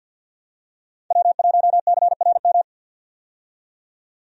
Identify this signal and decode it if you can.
Morse code, heard as a keyed tone, about 52 wpm, 700 Hz. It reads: W1CRM
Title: USA Call Signs - 5 Characters - 50wpm